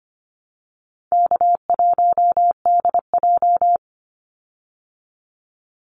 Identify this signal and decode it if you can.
Morse code, heard as a keyed tone, about 25 wpm, 700 Hz. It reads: K1DJ
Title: Easy Call Signs Repeated 2x - 25wpm